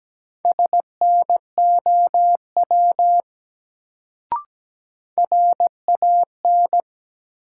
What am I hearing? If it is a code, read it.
Morse code, heard as a keyed tone, about 17 wpm, 700 Hz. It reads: SNOW RAN
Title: Sentences from Top 500 Words - New Word Drill - 17wpm